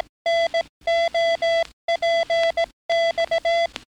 Morse code, heard as a keyed tone, about 18 wpm, 670 Hz, with noise. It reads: NOPX
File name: training / NAVPERS 11049to11060 / NAVPERS 11058 RA RB